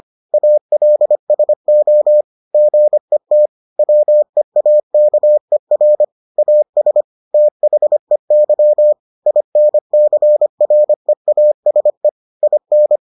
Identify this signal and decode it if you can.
Morse code, heard as a keyed tone, about 25 wpm, 600 Hz. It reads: ALSO GET WEAKER AS THEY INCREASE IN